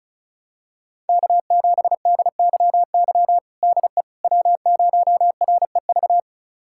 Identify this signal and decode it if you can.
Morse code, heard as a keyed tone, about 34 wpm, 700 Hz. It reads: K7DYY DE W0REV
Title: USA Callsign DE Callsign - 5 Characters - 35wpm